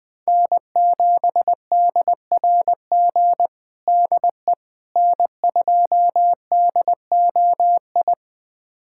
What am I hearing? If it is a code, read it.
Morse code, heard as a keyed tone, about 20 wpm, 700 Hz. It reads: N7DRG DE N2DOI
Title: USA Callsign DE Callsign - 5 Characters - 20wpm